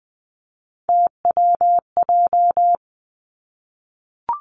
Morse code, heard as a keyed tone, about 20 wpm, 700 Hz. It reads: TWJ
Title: Three Challenging Letter-Number Combinations 20wpm